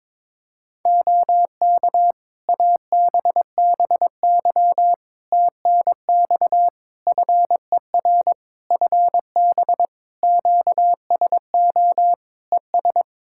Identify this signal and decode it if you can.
Morse code, heard as a keyed tone, about 22 wpm, 700 Hz. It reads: OK ABBY TNX FER FB QSO ES
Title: Practice QSO - Exchange 4 - Ending - 22wpm